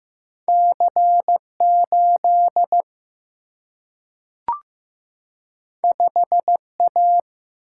Morse code, heard as a keyed tone, about 15 wpm, 700 Hz. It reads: C 8 5 A
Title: Single-Letter Single-Number 15wpm